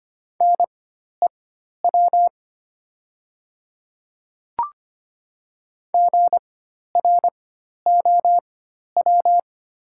Morse code, heard as a keyed tone, about 25 wpm, spaced as for 12 wpm, 690 Hz. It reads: NEW GROW
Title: 4x Character Spacing - Top 300 Words - 25wpm